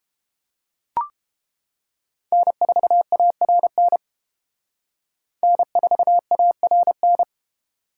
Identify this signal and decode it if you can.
Morse code, heard as a keyed tone, about 28 wpm, 700 Hz. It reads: N4ARN N4ARN